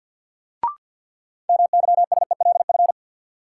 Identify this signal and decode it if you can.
Morse code, heard as a keyed tone, about 53 wpm, 700 Hz. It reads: MYSELF
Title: Sentences from Top 1300 Words - New Word Drill - 50wpm